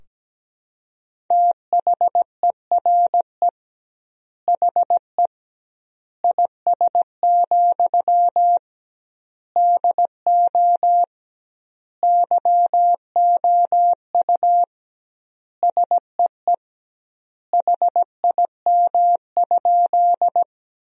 Morse code, heard as a keyed tone, about 17 wpm, 700 Hz. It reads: THERE HE IS, DO YOU SEE HIM?
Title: 2x Word-Spacing - Sentences from Top 100 Words - 17wpm